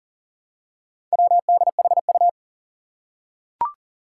Morse code, heard as a keyed tone, about 40 wpm, 700 Hz. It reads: WDHU
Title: Four Challenging Letter-Number Combinations 40wpm